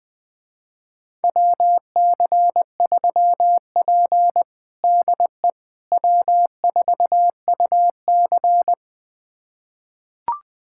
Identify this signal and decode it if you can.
Morse code, heard as a keyed tone, about 20 wpm, 700 Hz. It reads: WC3P DE W4UC
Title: USA Callsign DE Callsign - 4 Characters - 20wpm